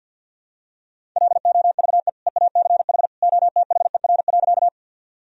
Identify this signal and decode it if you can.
Morse code, heard as a keyed tone, about 50 wpm, 700 Hz. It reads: LOVE EACH OTHER.
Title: Sentences from Top 1000 Words - Review Part 1 - 50wpm